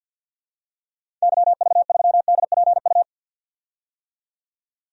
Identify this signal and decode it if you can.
Morse code, heard as a keyed tone, about 51 wpm, 700 Hz. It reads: YV3DPU
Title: DXCC-100 Call Signs - 6 Characters - 50wpm